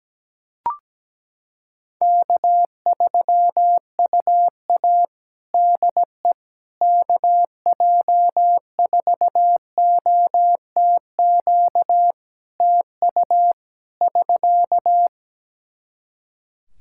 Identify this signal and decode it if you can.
Morse code, heard as a keyed tone, about 17 wpm, 700 Hz. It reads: K3UA DE KJ4OTQ TU <SK>
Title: Practice QSO - Exchange 4 - Ending - 17wpm